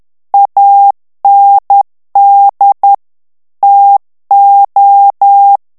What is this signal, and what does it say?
Morse code, heard as a keyed tone, about 10 wpm, 800 Hz. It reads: AND TO
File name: ANGLAIS texte clair 3 vitesse 10 mots minute